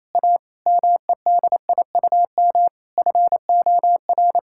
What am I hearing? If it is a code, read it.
Morse code, heard as a keyed tone, about 28 wpm, 700 Hz. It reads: A MEDIUM FOR
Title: Sentences from Top 1000 Words - Review Part 6 - 28wpm